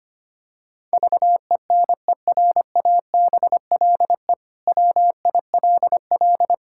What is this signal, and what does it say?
Morse code, heard as a keyed tone, about 25 wpm, 700 Hz. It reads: VENERABLE WILL
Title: Pangram Challenge v2 - 25wpm